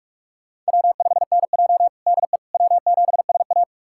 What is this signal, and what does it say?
Morse code, heard as a keyed tone, about 44 wpm, 700 Hz. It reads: W5NJ DE W7SA
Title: USA Callsign DE Callsign - 4 Characters - 45wpm